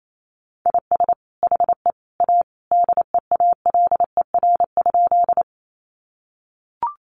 Morse code, heard as a keyed tone, about 28 wpm, 700 Hz. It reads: IS HE A DEALER?
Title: Sentences from Top 1100 Words 28wpm